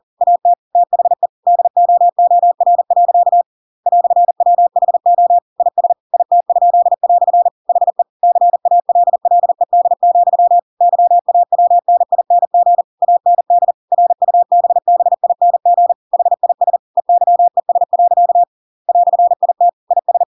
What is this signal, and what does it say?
Morse code, heard as a keyed tone, about 40 wpm, 700 Hz. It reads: AT THE DOOR. "WHO IS IT?" HE CALLED, YAWNING AND RUBBING HIS EYES. "IT IS
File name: Pinocchio / Pinocchio ch6 40 40 ik1hge